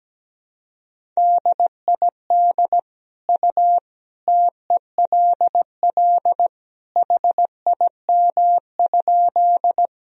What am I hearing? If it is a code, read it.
Morse code, heard as a keyed tone, about 17 wpm, 700 Hz. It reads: DID U TELL HIM?
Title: Sentences from Top 200 Words and Abbreviations 17wpm